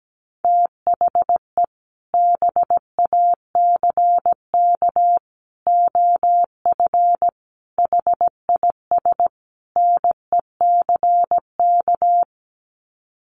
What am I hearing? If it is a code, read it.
Morse code, heard as a keyed tone, about 17 wpm, 700 Hz. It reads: THE BACK OF HIS NECK
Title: Sets of 5 Words Encore 17wpm